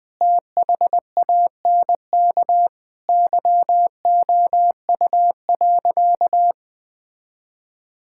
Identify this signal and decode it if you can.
Morse code, heard as a keyed tone, about 20 wpm, 700 Hz. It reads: THANK YOU.
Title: Sentences with 8-Word Phrases - 20wpm